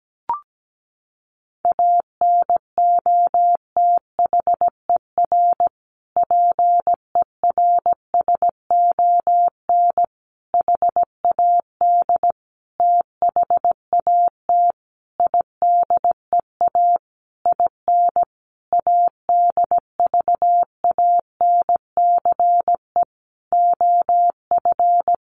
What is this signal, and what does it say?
Morse code, heard as a keyed tone, about 17 wpm, 700 Hz. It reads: ANOTHER PERSON HAD THAT IDEA IN ADVANCE OF
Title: Sentences from Top 1400 Words 17wpm